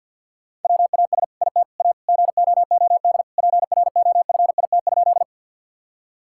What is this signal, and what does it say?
Morse code, heard as a keyed tone, about 50 wpm, 700 Hz. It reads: WAS IT A GOOD PROFIT?
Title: Sentences from Top 1000 Words - Review Part 5 - 50wpm